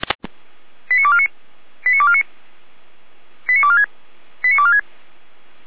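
Five-tone selective call (ZVEI), heard as quick runs of five tones, tones of about 70 ms. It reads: 89249 89249 89246 89246
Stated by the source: ZVEI five-tone call received by SDR.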